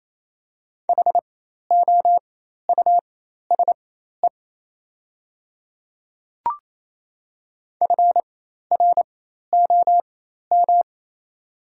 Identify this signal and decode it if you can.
Morse code, heard as a keyed tone, about 28 wpm, spaced as for 13 wpm, 700 Hz. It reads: HOUSE FROM
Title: 4x Character Spacing - Top 200 Words - 28wpm